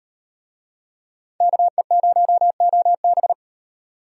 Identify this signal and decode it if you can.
Morse code, heard as a keyed tone, about 38 wpm, 700 Hz. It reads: KE0OB